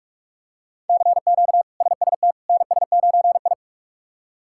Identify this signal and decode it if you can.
Morse code, heard as a keyed tone, about 46 wpm, 700 Hz. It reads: CQ SST NS9I
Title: K1USN SST QSO Practice - 45wpm